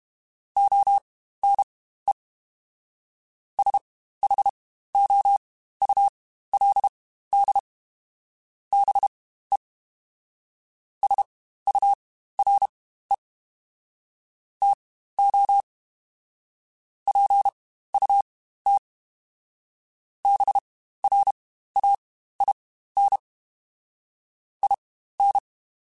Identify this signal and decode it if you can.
Morse code, heard as a keyed tone, about 32 wpm, spaced as for 15 wpm, 800 Hz. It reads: ONE SHOULD BE SURE TO PUT BRAIN IN
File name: Spaced / Chuckles Spaced-mp3s / BrainMouth sp @30